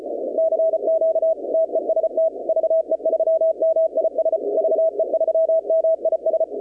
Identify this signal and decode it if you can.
Morse code, heard as a keyed tone, about 33 wpm, 650 Hz. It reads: CQ TEST VE3MIS VE3MIS
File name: contests / 2019-cw-wpx-kt5j-k5pi- -2330 0-21